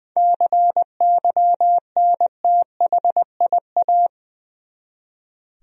Morse code, heard as a keyed tone, about 20 wpm, 700 Hz. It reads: CYNTHIA
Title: USA Common Names - 20wpm